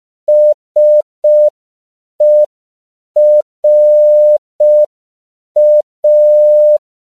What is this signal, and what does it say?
Morse code, heard as a keyed tone, about 5 wpm, 600 Hz. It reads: SERA